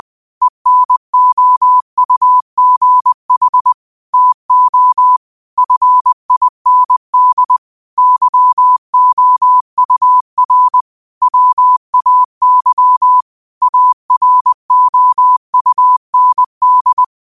Morse code, heard as a keyed tone, about 20 wpm, 1000 Hz. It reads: ENOUGH TO FIND YOUR WAY AROUND
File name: qotdfiles / 20wpm